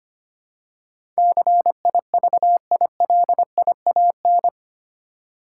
Morse code, heard as a keyed tone, about 25 wpm, 700 Hz. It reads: CIVILIAN
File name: 8 Letter Words - First 3 Letters Given - 25wpm